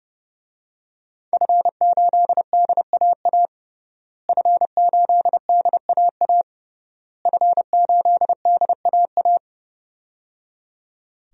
Morse code, heard as a keyed tone, about 30 wpm, 700 Hz. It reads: F8DAA F8DAA F8DAA
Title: DXCC-100 Call Signs - 5 Characters - Repeated 3x - 30wpm